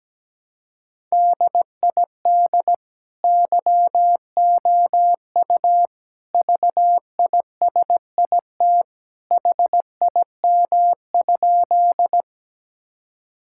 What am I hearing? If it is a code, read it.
Morse code, heard as a keyed tone, about 17 wpm, 700 Hz. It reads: DID YOU VISIT HIM?
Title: Sentences from Top 1000 Words - Review Part 6 - 17wpm